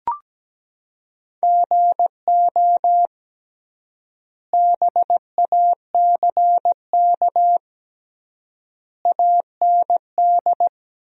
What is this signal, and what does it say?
Morse code, heard as a keyed tone, about 17 wpm, 700 Hz. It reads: GO BACK AND
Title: 3x Word-Spacing - Sentences from Top 300 Words - 17wpm